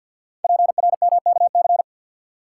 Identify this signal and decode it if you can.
Morse code, heard as a keyed tone, about 50 wpm, 700 Hz. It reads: PRMKC
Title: Code Groups - 5 Letters - 50wpm